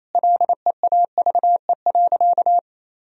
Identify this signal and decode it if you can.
Morse code, heard as a keyed tone, about 28 wpm, 700 Hz. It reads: LEAVE.
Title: Sentences from Top 1000 Words - Review Part 4 - 28wpm